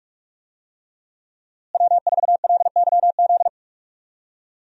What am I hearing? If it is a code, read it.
Morse code, heard as a keyed tone, about 41 wpm, 700 Hz. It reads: W4LYZ
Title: USA Call Signs - 5 Characters - Repeated 3x - 45wpm